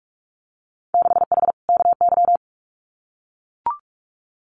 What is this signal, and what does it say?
Morse code, heard as a keyed tone, about 44 wpm, 700 Hz. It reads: 6H KY
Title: Winter Field Day Exchanges Encore 45wpm